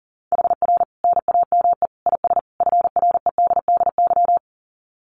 Morse code, heard as a keyed tone, about 40 wpm, 700 Hz. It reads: HR NAME IS FREDDY
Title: Common QSO Phrases 40wpm